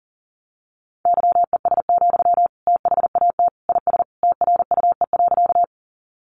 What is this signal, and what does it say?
Morse code, heard as a keyed tone, about 40 wpm, 700 Hz. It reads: YES, THAT IS TRUE.
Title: Sentences from Top 500 Words 40wpm